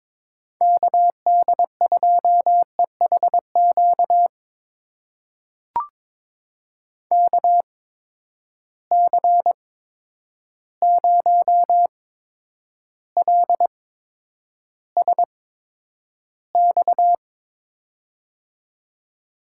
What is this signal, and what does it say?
Morse code, heard as a keyed tone, about 22 wpm, spaced as for 9 wpm, 700 Hz. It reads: KD2EHQ KC0LSX
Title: USA Call Signs - 6 Characters - 8x Character Spacing - 22wpm